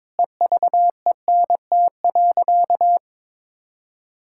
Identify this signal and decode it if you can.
Morse code, heard as a keyed tone, about 22 wpm, 700 Hz. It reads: EVENT.
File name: Difficult Sentences with Hints - 22wpm